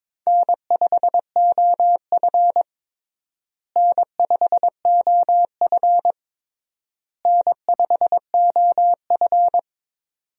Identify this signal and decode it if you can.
Morse code, heard as a keyed tone, about 22 wpm, 700 Hz. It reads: N5OF N5OF N5OF
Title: USA Call Signs - 4 Characters - Repeated 3x - 22wpm